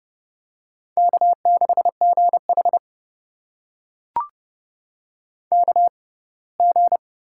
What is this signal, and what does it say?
Morse code, heard as a keyed tone, about 31 wpm, spaced as for 21 wpm, 700 Hz. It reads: K6GH KG
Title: USA Call Signs - 6x Character Spacing - 30wpm